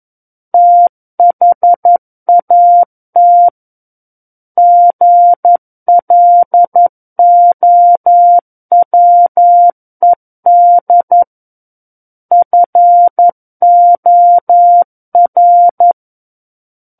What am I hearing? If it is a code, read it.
Morse code, heard as a keyed tone, about 11 wpm, 700 Hz. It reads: THAT GLOWED FOR
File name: Book One / 02-Chapter 2 - 11 WPM